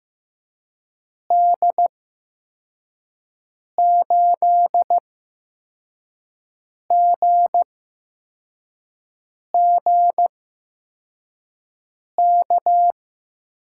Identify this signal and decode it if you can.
Morse code, heard as a keyed tone, about 15 wpm, spaced as for 4 wpm, 700 Hz. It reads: D8GGK
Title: USA Call Signs - 6 Characters - 8x Character Spacing - 15wpm